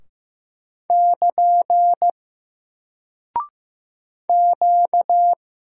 Morse code, heard as a keyed tone, about 15 wpm, 700 Hz. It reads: <KN> Q
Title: Single Character (Letter:Number:Punctuation:Prosign) Alt 15wpm